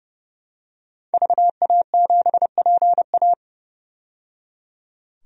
Morse code, heard as a keyed tone, about 29 wpm, 690 Hz. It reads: VA7PA
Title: DXCC-100 Call Signs - 5 Characters - 30wpm